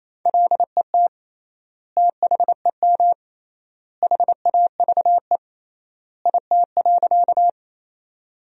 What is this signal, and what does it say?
Morse code, heard as a keyed tone, about 28 wpm, 700 Hz. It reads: LET THEM HAVE IT.
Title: 3x Word-Spacing - Sentences from Top 300 Words - 28wpm